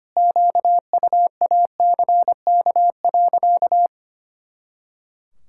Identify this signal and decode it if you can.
Morse code, heard as a keyed tone, about 25 wpm, 700 Hz. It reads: QUACK.
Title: Pangram Challenge v2 - 25wpm